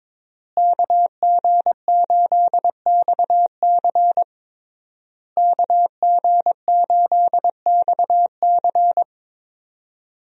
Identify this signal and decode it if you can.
Morse code, heard as a keyed tone, about 22 wpm, 700 Hz. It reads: KG8XC KG8XC
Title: USA Call Signs - 5 Characters - Repeated 3x - 22wpm